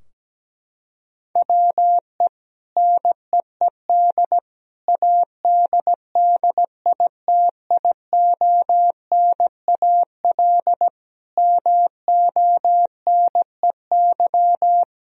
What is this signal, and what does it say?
Morse code, heard as a keyed tone, about 17 wpm, 700 Hz. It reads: WE NEED ADDITIONAL MONEY
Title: Sentences from Top 1000 Words - Review Part 4 - 17wpm